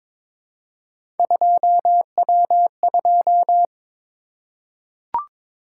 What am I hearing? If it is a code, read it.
Morse code, heard as a keyed tone, about 22 wpm, 700 Hz. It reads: 2W2
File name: practice / 22wpm / Three Letter-Number Combinations - 22wpm